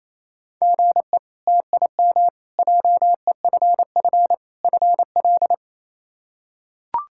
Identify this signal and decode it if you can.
Morse code, heard as a keyed tone, about 28 wpm, 700 Hz. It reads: GE TIM JEFF FL
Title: K1USN SST QSO Practice - 28wpm